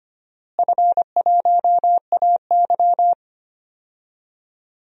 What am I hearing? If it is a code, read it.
Morse code, heard as a keyed tone, about 25 wpm, 700 Hz. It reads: F1AY
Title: DXCC-100 Call Signs - 4 Characters - Repeated 3x - 25wpm